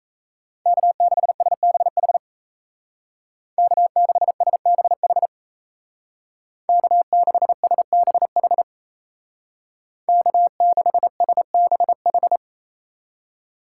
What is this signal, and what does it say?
Morse code, heard as a keyed tone, about 33 wpm, 700 Hz. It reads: K6SBH K6SBH K6SBH K6SBH
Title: Speed Racing - USA Call Signs 28wpm